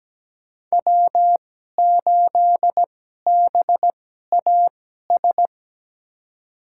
Morse code, heard as a keyed tone, about 17 wpm, spaced as for 12 wpm, 700 Hz. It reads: W8BAS
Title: USA Call Signs - 2x Character Spacing - 17wpm